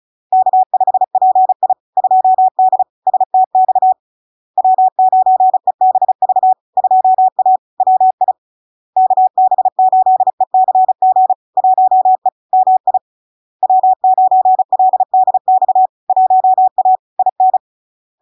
Morse code, heard as a keyed tone, about 36 wpm, 750 Hz. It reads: K5PI 2D STX W9EBV 2A WI KB8ECG 1E MI W9LDX 1A IN
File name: LARK-FD-2022-Log-in-CW-35wpm-750-Hz